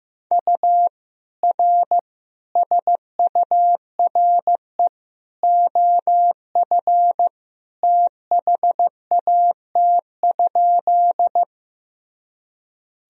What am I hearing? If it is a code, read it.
Morse code, heard as a keyed tone, about 16 wpm, 700 Hz. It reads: U R SURE OF THAT?
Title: Sentences from Top 300 Words and Abbreviations 15wpm